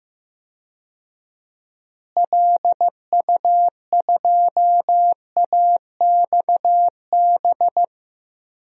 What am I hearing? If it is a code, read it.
Morse code, heard as a keyed tone, about 15 wpm, 700 Hz. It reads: LU2AXB
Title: DXCC-100 Call Signs - 6 Characters - Repeated 3x - 15wpm